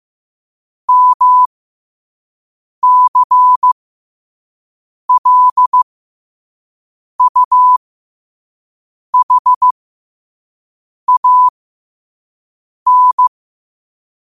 Morse code, heard as a keyed tone, about 15 wpm, spaced as for 5 wpm, 1000 Hz. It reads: MCLUHAN
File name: qotdfiles / 5wpm